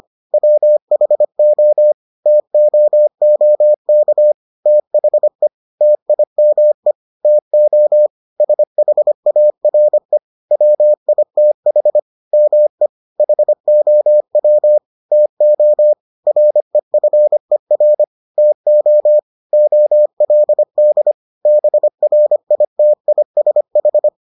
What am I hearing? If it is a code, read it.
Morse code, heard as a keyed tone, about 25 wpm, 600 Hz. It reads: WHO TOOK THE TIME TO SHARE WITH ME HOW TO REFER TO OLD BRITISH